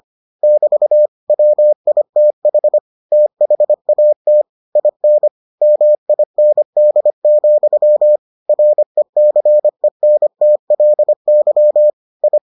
Morse code, heard as a keyed tone, about 25 wpm, 600 Hz. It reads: <BT> WITH THAT IN MIND, RECENTLY I